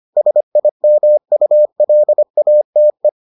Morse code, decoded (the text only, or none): SIMULATE